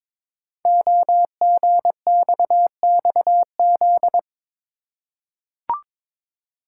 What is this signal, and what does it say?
Morse code, heard as a keyed tone, about 22 wpm, 700 Hz. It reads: OGXXZ
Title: Code Groups - 5 Letter:Numbers - 22wpm